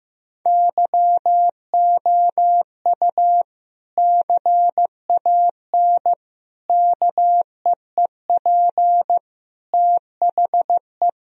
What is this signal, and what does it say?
Morse code, heard as a keyed tone, about 15 wpm, 700 Hz. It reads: YOU CAN KEEP THE
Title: Sentences from Top 1000 Words - Review Part 1 - 15wpm